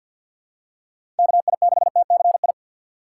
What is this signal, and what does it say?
Morse code, heard as a keyed tone, about 47 wpm, 690 Hz. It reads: KI6TCI
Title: USA Call Signs - 6x Character Spacing - 50wpm